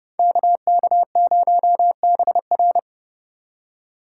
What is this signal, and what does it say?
Morse code, heard as a keyed tone, about 35 wpm, 700 Hz. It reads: KK0BR